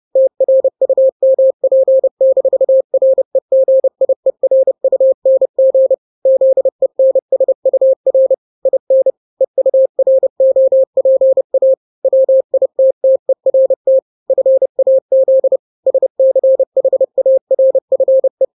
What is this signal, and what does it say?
Morse code, heard as a keyed tone, about 30 wpm, 530 Hz. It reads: TRUMP-REGIERUNG ZENSUR IN EUROPA WITTERT FAZ SCHARFE
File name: 29wpm